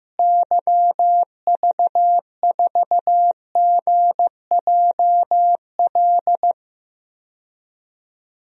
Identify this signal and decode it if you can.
Morse code, heard as a keyed tone, about 15 wpm, 700 Hz. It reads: YV4GJL
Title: DXCC-100 Call Signs - 6 Characters - Repeated 2x - 15wpm